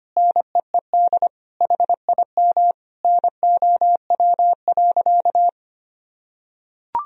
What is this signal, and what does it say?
Morse code, heard as a keyed tone, about 25 wpm, 700 Hz. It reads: NEED HIM NOW.
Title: Sentences from Top 1000 Words - Review Part 1 - 25wpm